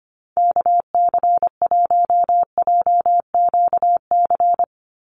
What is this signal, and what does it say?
Morse code, heard as a keyed tone, about 25 wpm, 700 Hz. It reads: KC1JQC
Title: USA Call Signs 25wpm